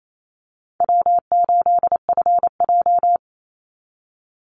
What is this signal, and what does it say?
Morse code, heard as a keyed tone, about 28 wpm, 700 Hz. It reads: W8FJ
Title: Easy Call Signs Repeated 2x - 28wpm